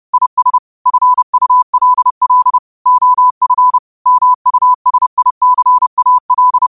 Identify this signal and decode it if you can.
Morse code, heard as a keyed tone, about 30 wpm, 1000 Hz. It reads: IS FULL OF MUSICAL
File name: qotdfiles / 30wpm